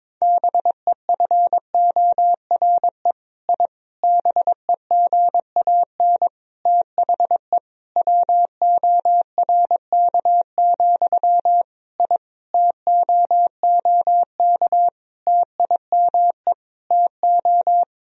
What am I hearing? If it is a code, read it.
Morse code, heard as a keyed tone, about 22 wpm, 700 Hz. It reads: BEFORE I BEGAN THE WORK, I TOOK TIME TO
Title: Sentences from Top 1000 Words - Review Part 4 - 22wpm